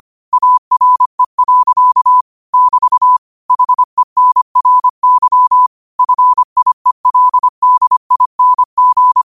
Morse code, heard as a keyed tone, about 25 wpm, 1000 Hz. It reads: ARE. <BT> HENRY FIELDING
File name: qotdfiles / 25wpm